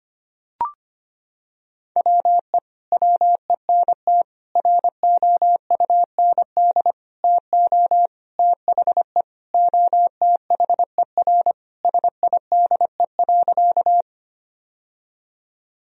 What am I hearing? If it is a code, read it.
Morse code, heard as a keyed tone, about 25 wpm, 700 Hz. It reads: WE WENT ROUND TO THE OTHER SIDE.
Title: Sentences from Top 1000 Words - Review Part 1 - 25wpm